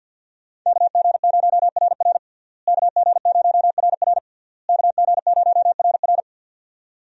Morse code, heard as a keyed tone, about 52 wpm, 700 Hz. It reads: KG0RR KG0RR KG0RR
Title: USA Call Signs - 5 Characters - Repeated 3x - 50wpm